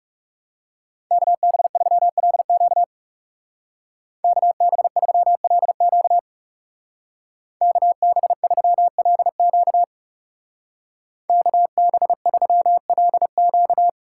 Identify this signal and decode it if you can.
Morse code, heard as a keyed tone, about 36 wpm, 700 Hz. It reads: KB3LQ KB3LQ KB3LQ KB3LQ
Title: Speed Racing - USA 2x2 Call Signs 30wpm